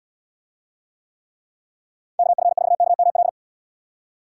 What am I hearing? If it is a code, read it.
Morse code, heard as a keyed tone, about 72 wpm, 700 Hz. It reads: KF4KML